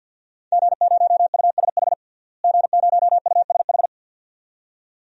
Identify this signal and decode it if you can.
Morse code, heard as a keyed tone, about 52 wpm, 700 Hz. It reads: G0USH G0USH
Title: DXCC-100 Call Signs - 5 Characters - Repeated 2x - 50wpm